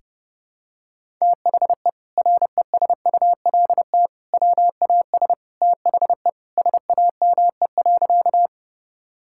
Morse code, heard as a keyed tone, about 30 wpm, 700 Hz. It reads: THE RESULT WAS THE SAME.
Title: Sentences from Top 1000 Words - Review Part 5 - 30wpm